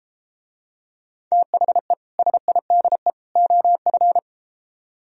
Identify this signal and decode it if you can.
Morse code, heard as a keyed tone, about 33 wpm, 700 Hz. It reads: THE SIDE OF